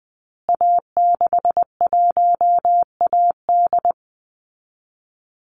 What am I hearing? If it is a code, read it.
Morse code, heard as a keyed tone, about 20 wpm, 700 Hz. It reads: A61AD
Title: Call Signs Repeated 3x - 20wpm